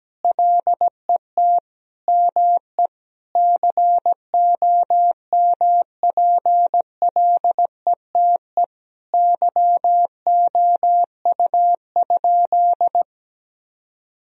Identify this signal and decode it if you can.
Morse code, heard as a keyed tone, about 17 wpm, 700 Hz. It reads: LET ME COMPLETE YOU?
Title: Sentences from Top 1000 Words - Review Part 2 - 17wpm